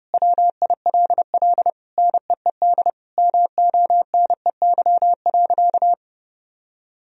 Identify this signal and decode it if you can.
Morse code, heard as a keyed tone, about 29 wpm, 700 Hz. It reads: WILL NEED MONEY.
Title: Sentences from Top 1000 Words - Review Part 4 - 30wpm